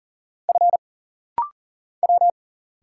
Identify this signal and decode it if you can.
Morse code, heard as a keyed tone, about 40 wpm, 700 Hz. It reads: F W
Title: Single Letters - Rapid-Fire 40wpm